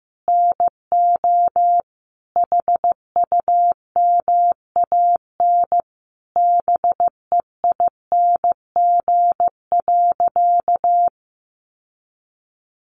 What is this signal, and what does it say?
Morse code, heard as a keyed tone, about 15 wpm, 700 Hz. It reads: NO HUMAN BEING.
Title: Sentences from Top 1500 Words 15wpm